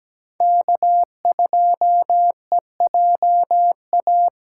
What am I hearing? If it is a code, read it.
Morse code, heard as a keyed tone, about 17 wpm, 700 Hz. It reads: K2EJA
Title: USA Call Signs - 5 Characters - 6x Character Spacing - 17wpm